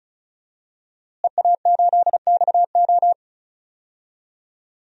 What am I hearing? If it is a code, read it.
Morse code, heard as a keyed tone, about 36 wpm, 700 Hz. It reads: EA8XO
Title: DXCC-100 Call Signs - 5 Characters - Repeated 3x - 35wpm